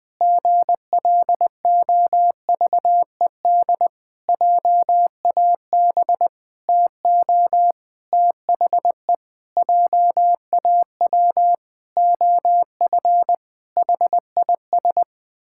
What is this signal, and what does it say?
Morse code, heard as a keyed tone, about 20 wpm, 700 Hz. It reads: GLOVED JAB TO THE JAW OF HIS
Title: Pangram Challenge v2 - 20wpm